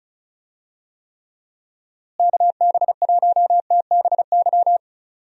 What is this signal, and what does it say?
Morse code, heard as a keyed tone, about 36 wpm, 700 Hz. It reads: KB1TBY
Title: USA Call Signs - 6 Characters - Repeated 3x - 35wpm